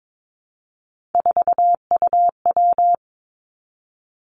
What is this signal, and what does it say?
Morse code, heard as a keyed tone, about 22 wpm, 700 Hz. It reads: 4UW
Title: Three Challenging Letter-Number Combinations 22wpm